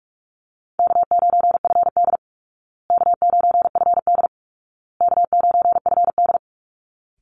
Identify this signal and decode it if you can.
Morse code, heard as a keyed tone, about 44 wpm, 700 Hz. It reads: K9FD K9FD K9FD
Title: Call Signs Repeated 2x - 45wpm